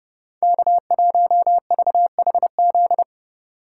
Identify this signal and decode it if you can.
Morse code, heard as a keyed tone, about 30 wpm, 700 Hz. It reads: K1VHZ
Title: USA Call Signs - 2x Character Spacing - 30wpm